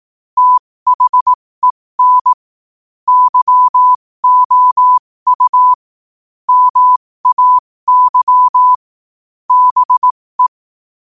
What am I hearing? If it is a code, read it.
Morse code, heard as a keyed tone, about 18 wpm, spaced as for 15 wpm, 1000 Hz. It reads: THEN YOU MAY BE
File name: qotdfiles / 15wpm